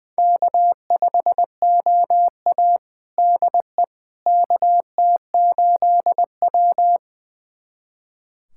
Morse code, heard as a keyed tone, about 20 wpm, 700 Hz. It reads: K5OA DE KT8W
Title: USA Callsign DE Callsign - 4 Characters - 20wpm